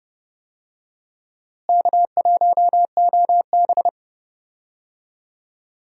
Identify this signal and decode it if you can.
Morse code, heard as a keyed tone, about 31 wpm, 700 Hz. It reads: K1OB
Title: USA Call Signs - 4 Characters - Repeated 2x - 30wpm